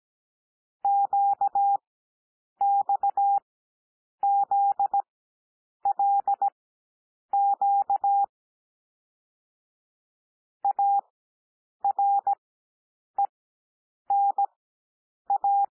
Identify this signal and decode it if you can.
Morse code, heard as a keyed tone, about 17 wpm, spaced as for 8 wpm, 800 Hz. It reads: QXZLQ ARENA
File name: cwmp3 / L2 SOUND LETTERS 17 wpm